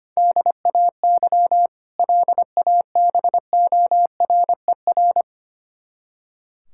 Morse code, heard as a keyed tone, about 25 wpm, 700 Hz. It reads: DAY LABORER
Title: K1USN SST QSO Practice 25wpm